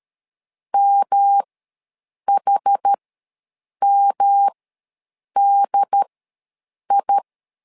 Morse code, heard as a keyed tone, about 13 wpm, spaced as for 7 wpm, 800 Hz. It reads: MHMDI
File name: LEZIONI / 3 MWRO / 3 M ws 12wpm 1745